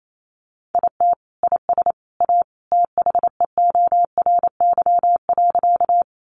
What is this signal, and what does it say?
Morse code, heard as a keyed tone, about 28 wpm, 700 Hz. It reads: IT IS A THEORY.
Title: Sentences from Top 600 Words 28wpm